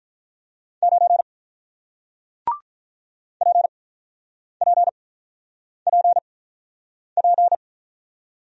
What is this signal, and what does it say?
Morse code, heard as a keyed tone, about 48 wpm, spaced as for 9 wpm, 700 Hz. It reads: 9 PPPP